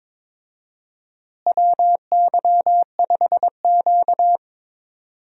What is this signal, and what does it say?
Morse code, heard as a keyed tone, about 22 wpm, 700 Hz. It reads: WY5Q
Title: USA Call Signs - 4 Characters - Repeated 2x - 22wpm